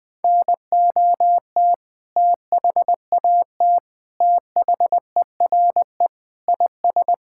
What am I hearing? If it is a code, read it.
Morse code, heard as a keyed tone, about 20 wpm, 700 Hz. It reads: NOT THAT THERE IS
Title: Sentences with 7-Word Phrases - 20wpm